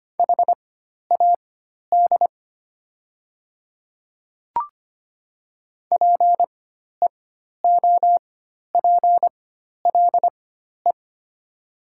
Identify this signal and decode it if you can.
Morse code, heard as a keyed tone, about 25 wpm, spaced as for 12 wpm, 700 Hz. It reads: HAD PEOPLE
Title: 4x Character Spacing - Top 200 Words - 25wpm